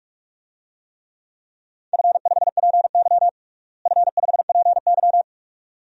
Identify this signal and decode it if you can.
Morse code, heard as a keyed tone, about 46 wpm, 700 Hz. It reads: F5PY F5PY
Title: DXCC-100 Call Signs - 4 Characters - Repeated 2x - 45wpm